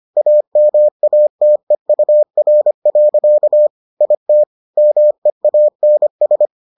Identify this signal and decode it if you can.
Morse code, heard as a keyed tone, about 25 wpm, 600 Hz. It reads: AMATEUR. IT MEANS